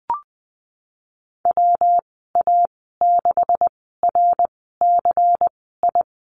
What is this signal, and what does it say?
Morse code, heard as a keyed tone, about 20 wpm, spaced as for 15 wpm, 700 Hz. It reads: WA6RCI
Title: 179 Lesson 29g - Slash - Call Signs 20wpm